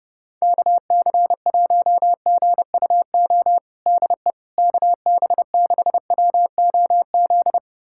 Morse code, heard as a keyed tone, about 30 wpm, 700 Hz. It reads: KC1GUO DE KB6WOZ
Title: USA Callsign DE Callsign 30wpm